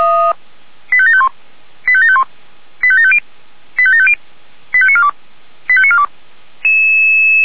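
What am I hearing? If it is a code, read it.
Five-tone selective call (ZVEI), heard as quick runs of five tones, tones of about 70 ms. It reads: 85751 85751 85750 85750 85942 85942
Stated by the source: Five-tone ZVEI selective call 85730 received on an SDR.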